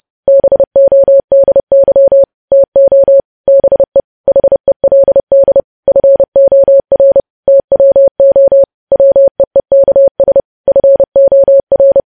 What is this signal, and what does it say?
Morse code, heard as a keyed tone, about 30 wpm, 550 Hz. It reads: BODY TO BE HELD FOR TWO WEEKS FOR
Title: News Headlines at 30